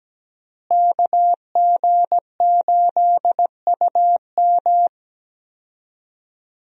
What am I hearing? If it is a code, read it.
Morse code, heard as a keyed tone, about 17 wpm, 700 Hz. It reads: KG8UM
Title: USA Call Signs - 5 Characters - Repeated 2x - 17wpm